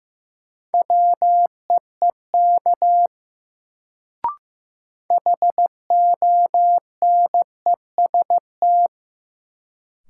Morse code, heard as a keyed tone, about 15 wpm, 700 Hz. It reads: WEEK HONEST
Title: Sets of 5 Words Encore - New Word Drill - 15wpm